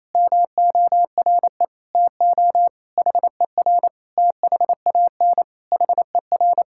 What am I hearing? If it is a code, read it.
Morse code, heard as a keyed tone, about 28 wpm, 700 Hz. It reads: MORE TO HER THAN HER
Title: Sentences from Top 1000 Words - Review Part 6 - 28wpm